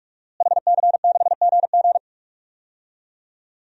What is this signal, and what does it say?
Morse code, heard as a keyed tone, about 44 wpm, 700 Hz. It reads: HC6GG
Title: DXCC-100 Call Signs - 5 Characters - Repeated 2x - 45wpm